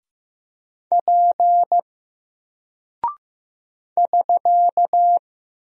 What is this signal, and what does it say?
Morse code, heard as a keyed tone, about 15 wpm, 700 Hz. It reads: P <SK>
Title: Single Character (Letter:Number:Punctuation:Prosign) Alt 15wpm